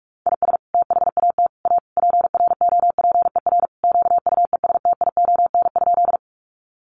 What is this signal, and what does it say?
Morse code, heard as a keyed tone, about 46 wpm, 700 Hz. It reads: IS THAT A PROPER QUESTION?
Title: Sentences from Top 1100 Words 45wpm